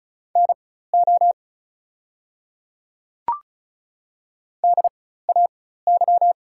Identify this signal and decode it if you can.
Morse code, heard as a keyed tone, about 34 wpm, spaced as for 16 wpm, 690 Hz. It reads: NO DAY
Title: 4x Character Spacing - Top 200 Words - 35wpm